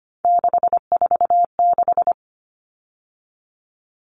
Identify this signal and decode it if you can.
Morse code, heard as a keyed tone, about 25 wpm, 690 Hz. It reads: 646
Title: Numbers - 3-Digit 25wpm